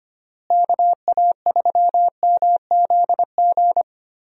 Morse code, heard as a keyed tone, about 25 wpm, 700 Hz. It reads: KA3MZG
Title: USA Call Signs - 6 Characters - Repeated 3x - 25wpm